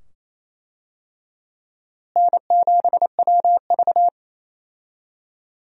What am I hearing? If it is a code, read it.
Morse code, heard as a keyed tone, about 27 wpm, 700 Hz. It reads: N7WV
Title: USA Call Signs - 4 Characters - Repeated 2x - 28wpm